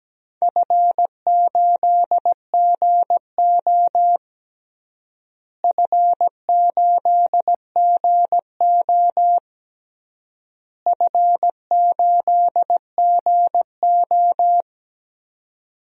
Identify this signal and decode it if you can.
Morse code, heard as a keyed tone, about 17 wpm, 700 Hz. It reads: F8GO F8GO F8GO
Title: DXCC-100 Call Signs - 4 Characters - Repeated 3x - 17wpm